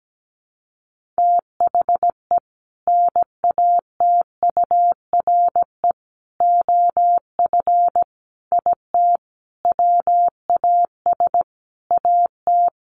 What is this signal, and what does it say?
Morse code, heard as a keyed tone, about 17 wpm, 700 Hz. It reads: THE NATURE OF IT WAS AT
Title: Sentences from Top 1400 Words 17wpm